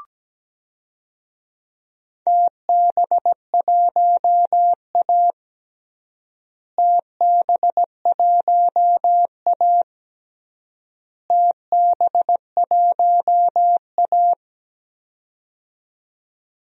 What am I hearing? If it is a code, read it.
Morse code, heard as a keyed tone, about 17 wpm, 700 Hz. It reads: TB1A TB1A TB1A
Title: DXCC-100 Call Signs - 4 Characters - Repeated 3x - 17wpm